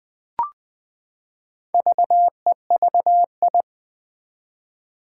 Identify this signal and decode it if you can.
Morse code, heard as a keyed tone, about 21 wpm, 700 Hz. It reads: VEVI
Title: Four Challenging Letter-Number Combinations 20wpm